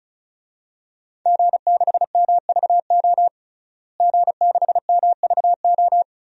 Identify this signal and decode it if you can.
Morse code, heard as a keyed tone, about 34 wpm, 700 Hz. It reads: G6MVO G6MVO
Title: DXCC-100 Call Signs - 5 Characters - Repeated 2x - 35wpm